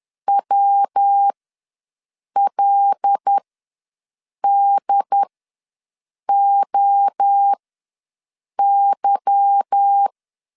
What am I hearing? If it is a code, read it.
Morse code, heard as a keyed tone, about 10 wpm, spaced as for 6 wpm, 800 Hz. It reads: WLDOY